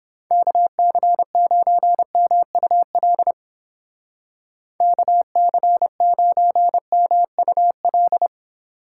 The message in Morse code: KC9MUL KC9MUL

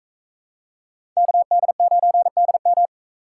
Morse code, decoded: KD9DM

